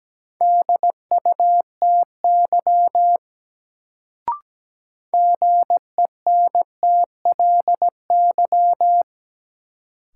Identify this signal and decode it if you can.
Morse code, heard as a keyed tone, about 17 wpm, 700 Hz. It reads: DUTY GENTLY
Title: Sentences from Top 1500 Words - New Word Drill - 17wpm